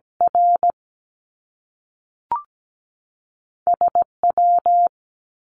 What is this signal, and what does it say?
Morse code, heard as a keyed tone, about 17 wpm, spaced as for 3 wpm, 700 Hz. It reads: RS W
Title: Top Two Letter Combinations in Words 17wpm